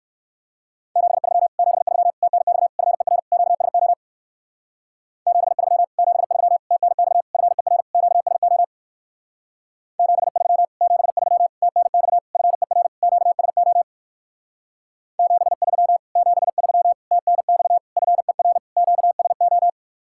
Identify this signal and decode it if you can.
Morse code, heard as a keyed tone, about 58 wpm, 700 Hz. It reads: 73 73 TNX FER QSO 73 73 TNX FER QSO 73 73 TNX FER QSO 73 73 TNX FER QSO